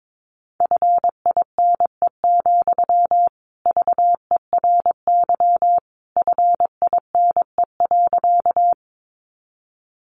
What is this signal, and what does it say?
Morse code, heard as a keyed tone, about 22 wpm, 700 Hz. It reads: FINE, VERY FINE.
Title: Sentences from Top 500 Words 22wpm